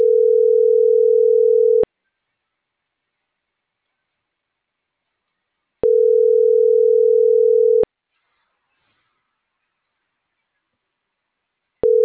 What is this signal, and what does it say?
Ringback tone (North America), 440 + 480 Hz.